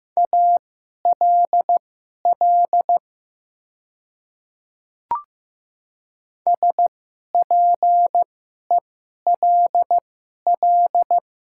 Morse code, heard as a keyed tone, about 15 wpm, spaced as for 11 wpm, 700 Hz. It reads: ALL SPELL
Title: 2x Character Spacing - Top 200 Words - 15wpm